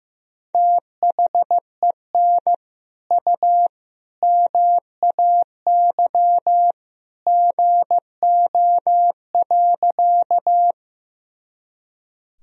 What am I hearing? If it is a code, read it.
Morse code, heard as a keyed tone, about 15 wpm, 700 Hz. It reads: THEN U MAY GO.
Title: Sentences from Top 100 Words and Abbreviations 15wpm